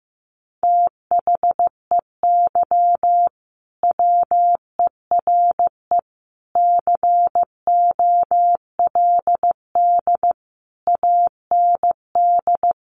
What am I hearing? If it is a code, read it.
Morse code, heard as a keyed tone, about 15 wpm, 700 Hz. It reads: THEY WERE COLD AND
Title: Sentences from Top 1500 Words 15wpm